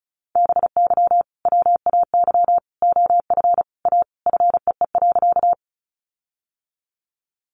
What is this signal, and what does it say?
Morse code, heard as a keyed tone, about 34 wpm, 700 Hz. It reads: BY WAY OF A FEE.
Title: Sentences from Top 1500 Words 35wpm